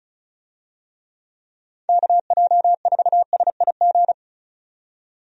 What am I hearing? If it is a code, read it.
Morse code, heard as a keyed tone, about 36 wpm, 700 Hz. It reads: KJ4SIG
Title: USA Call Signs - 6 Characters - 35wpm